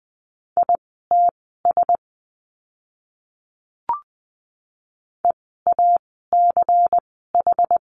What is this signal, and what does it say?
Morse code, heard as a keyed tone, about 20 wpm, spaced as for 15 wpm, 700 Hz. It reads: ITS EACH
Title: 094 Lesson 16f - Letter c - Words 20wpm